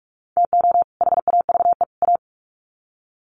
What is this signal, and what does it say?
Morse code, heard as a keyed tone, about 44 wpm, 700 Hz. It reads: TO HAVE A
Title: Sets of 3 Words 45wpm